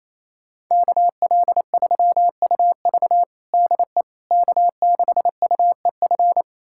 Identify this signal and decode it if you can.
Morse code, heard as a keyed tone, about 28 wpm, 700 Hz. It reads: KL3UV DE K6UEF
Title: USA Callsign DE Callsign - 5 Characters - 28wpm